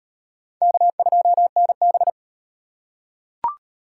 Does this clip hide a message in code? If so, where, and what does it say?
Yes: Morse code from 0.6 to 2.1 s, heard as a keyed tone, about 38 wpm, 700 Hz. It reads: K2NB